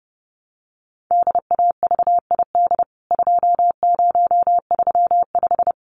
Morse code, heard as a keyed tone, about 30 wpm, 700 Hz. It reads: DAVID 2035
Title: CWT Exchange 30wpm